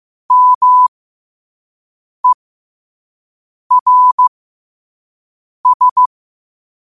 Morse code, heard as a keyed tone, about 15 wpm, spaced as for 5 wpm, 1000 Hz. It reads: MERS
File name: qotdfiles / 5wpm